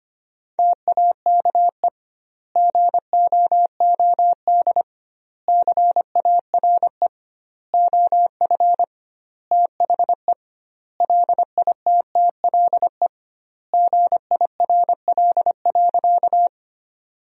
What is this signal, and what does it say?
Morse code, heard as a keyed tone, about 25 wpm, 700 Hz. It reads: TAKE GOOD CARE OF THE LITTLE GIRL.
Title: 2x Word-Spacing - Sentences from Top 400 Words - 25wpm